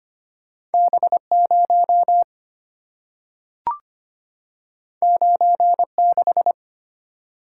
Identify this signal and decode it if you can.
Morse code, heard as a keyed tone, about 25 wpm, 690 Hz. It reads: B0 96
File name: Two Letter-Number Combinations - 25wpm